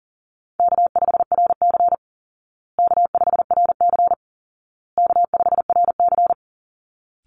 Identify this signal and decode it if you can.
Morse code, heard as a keyed tone, about 40 wpm, 700 Hz. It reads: K5RC K5RC K5RC
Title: Call Signs Repeated 2x - 40wpm